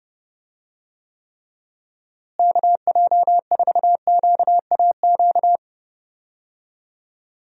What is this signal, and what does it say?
Morse code, heard as a keyed tone, about 32 wpm, 700 Hz. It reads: KJ4QAQ
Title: USA Call Signs - 6 Characters - Repeated 2x - 30wpm